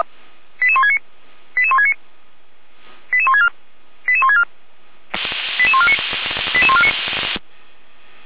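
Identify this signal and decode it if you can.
Five-tone selective call (ZVEI), heard as quick runs of five tones, tones of about 70 ms. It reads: 88169 88169 88164 88164 88149 88149
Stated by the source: ZVEI five-tone call received by SDR.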